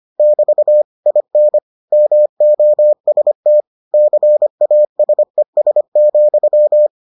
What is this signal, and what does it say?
Morse code, heard as a keyed tone, about 25 wpm, 600 Hz. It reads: <BT> IN MOST CASES,